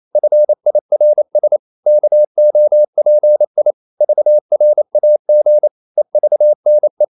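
Morse code, heard as a keyed tone, about 28 wpm, 600 Hz. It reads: FIRS KOPI VRAG EVNE
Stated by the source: Danish short words sent in Morse at 28 wpm, from CW med Gnister.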